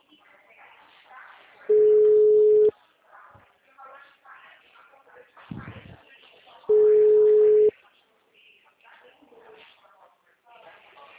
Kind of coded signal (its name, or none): telephone call progress tone